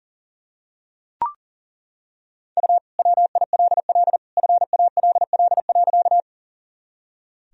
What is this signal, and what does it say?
Morse code, heard as a keyed tone, about 40 wpm, 700 Hz. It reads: U WILL FALL.
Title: Sentences from Top 500 Words and Abbreviations 40wpm